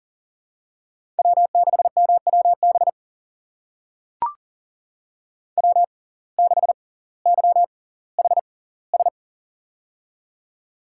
Morse code, heard as a keyed tone, about 39 wpm, spaced as for 20 wpm, 690 Hz. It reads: W6MWB W6YHS
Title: USA Call Signs - 5 Characters - 6x Character Spacing - 40wpm